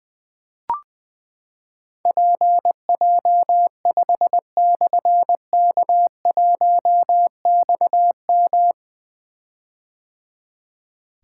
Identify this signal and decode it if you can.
Morse code, heard as a keyed tone, about 20 wpm, 700 Hz. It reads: PJ5/K1XM
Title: Difficult Call Signs 20wpm